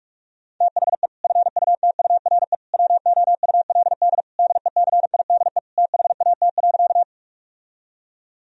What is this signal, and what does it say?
Morse code, heard as a keyed tone, about 46 wpm, 700 Hz. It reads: THE FUTURE WOULD DECIDE THAT.
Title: Sentences from Top 1000 Words - Review Part 3 - 45wpm